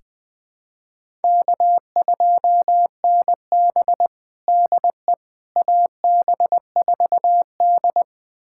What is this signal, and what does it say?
Morse code, heard as a keyed tone, about 21 wpm, 700 Hz. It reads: K2NB DE AB4D
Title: USA Callsign DE Callsign - 4 Characters - 20wpm